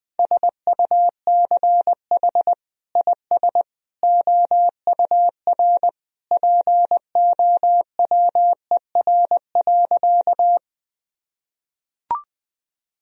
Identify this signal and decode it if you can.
Morse code, heard as a keyed tone, about 20 wpm, 700 Hz. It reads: SUCH IS OUR POWER.
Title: Sentences from Top 1000 Words - Review Part 2 - 20wpm